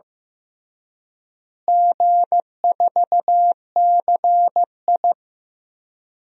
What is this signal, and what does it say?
Morse code, heard as a keyed tone, about 15 wpm, 700 Hz. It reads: G4CI
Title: DXCC-100 Call Signs - 4 Characters - Repeated 2x - 15wpm